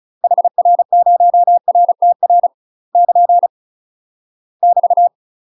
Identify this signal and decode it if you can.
Morse code, heard as a keyed tone, about 34 wpm, 700 Hz. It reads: HR0RTR <KN> <BT>
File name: UK MASTER QSOs TRAINING / 35WPM / UK MASTER QSOs 0043